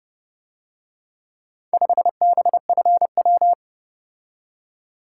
Morse code, heard as a keyed tone, about 30 wpm, 700 Hz. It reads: 5BFW